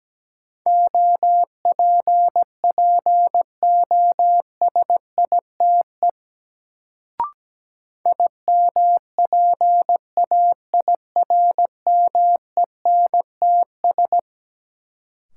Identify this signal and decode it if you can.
Morse code, heard as a keyed tone, about 17 wpm, 700 Hz. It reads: OPPOSITE IMPAIRMENTS
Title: Sets of 5 Words Encore - New Word Drill - 17wpm